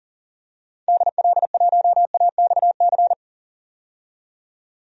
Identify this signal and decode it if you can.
Morse code, heard as a keyed tone, about 40 wpm, 700 Hz. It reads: DL1AXC
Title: DXCC-100 Call Signs - 6 Characters - 40wpm